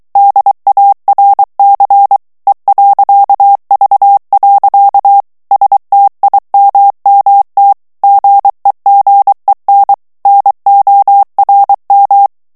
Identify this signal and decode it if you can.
Morse code, heard as a keyed tone, about 23 wpm, 800 Hz. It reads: DARC E.V. STIMMT GEGEN NORM
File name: ALLEMAND texte clair 1 vitesse 22 mots minute